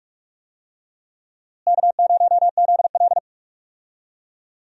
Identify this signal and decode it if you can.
Morse code, heard as a keyed tone, about 46 wpm, 700 Hz. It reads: K0ZL
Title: USA Call Signs - 4 Characters - Repeated 2x - 45wpm